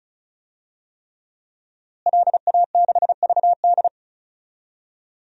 Morse code, heard as a keyed tone, about 34 wpm, 700 Hz. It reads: LA6VD
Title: DXCC-100 Call Signs - 5 Characters - Repeated 3x - 35wpm